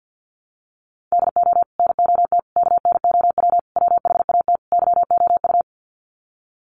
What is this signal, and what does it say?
Morse code, heard as a keyed tone, about 50 wpm, 700 Hz. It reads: DO NOT KNOW WHAT YOU
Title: Sets of 5 Words 50wpm